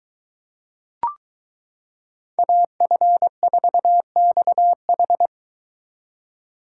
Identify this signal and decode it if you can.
Morse code, heard as a keyed tone, about 23 wpm, 700 Hz. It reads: AF4XH